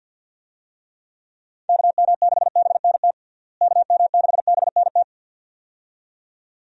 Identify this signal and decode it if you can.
Morse code, heard as a keyed tone, about 52 wpm, 700 Hz. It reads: KM6BNT KM6BNT
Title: USA Call Signs - 6 Characters - Repeated 2x - 50wpm